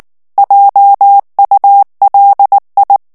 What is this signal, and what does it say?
Morse code, heard as a keyed tone, about 19 wpm, 800 Hz. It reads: JULI
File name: morse mp3 / ALLEMAND texte clair 2 vitesse 18 mots minute